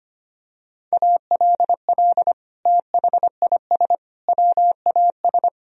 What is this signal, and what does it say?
Morse code, heard as a keyed tone, about 25 wpm, 700 Hz. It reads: ALL THIS WAS
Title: Sentences from Top 1000 Words - Review Part 2 - 25wpm